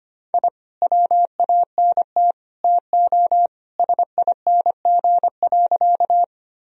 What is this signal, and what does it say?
Morse code, heard as a keyed tone, about 25 wpm, 700 Hz. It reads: I WANT TO SING.
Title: Sentences from Top 1000 Words - Review Part 2 - 25wpm